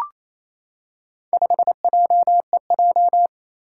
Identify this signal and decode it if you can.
Morse code, heard as a keyed tone, about 29 wpm, 700 Hz. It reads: 5JEJ
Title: Four Challenging Letter-Number Combinations 28wpm